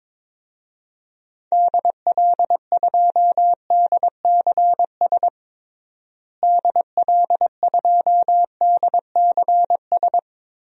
Morse code, heard as a keyed tone, about 22 wpm, 700 Hz. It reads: DL2DCS DL2DCS
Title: DXCC-100 Call Signs - 6 Characters - Repeated 3x - 22wpm